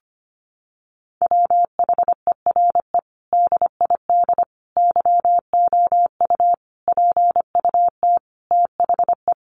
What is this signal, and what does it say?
Morse code, heard as a keyed tone, about 25 wpm, 700 Hz. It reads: WHERE DID YOU PUT THE
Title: Sentences from Top 1200 Words 25wpm